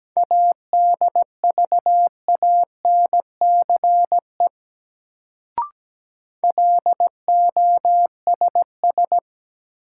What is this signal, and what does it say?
Morse code, heard as a keyed tone, about 17 wpm, 700 Hz. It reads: ADVANCE LOSS
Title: Sentences from Top 1400 Words - New Word Drill - 17wpm